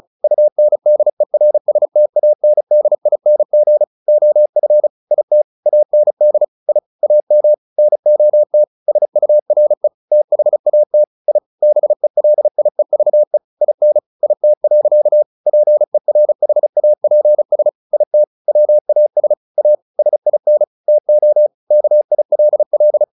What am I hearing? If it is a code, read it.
Morse code, heard as a keyed tone, about 36 wpm, 600 Hz. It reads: UNDERSTANDING OF IT AND I AM NOT SURE THAT I BELIEVE IN IT. PERHAPS IT WAS A SIN TO KILL
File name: the old man and the sea 2 35wpm